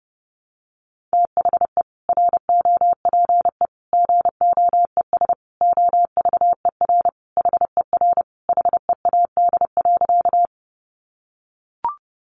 Morse code, heard as a keyed tone, about 30 wpm, 700 Hz. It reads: THE ROPE GOES OVER HER HEAD.
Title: Easy Phrases 30wpm